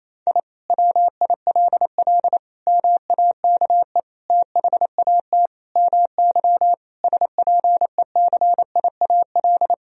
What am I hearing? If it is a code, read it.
Morse code, heard as a keyed tone, about 28 wpm, 700 Hz. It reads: I WILL MAKE THAT MY SPECIAL
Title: Sentences from Top 1000 Words - Review Part 3 - 28wpm